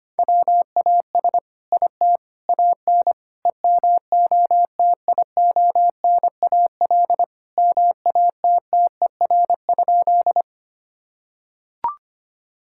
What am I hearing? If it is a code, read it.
Morse code, heard as a keyed tone, about 25 wpm, 700 Hz. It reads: WAS IT AN EMOTIONAL MATTER?
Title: Sentences from Top 1000 Words - Review Part 5 - 25wpm